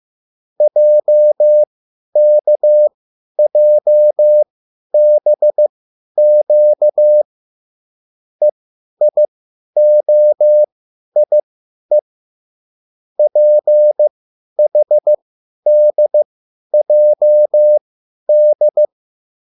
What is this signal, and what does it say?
Morse code, heard as a keyed tone, about 15 wpm, spaced as for 10 wpm, 600 Hz. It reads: JKJBQ EIOIE PHDJD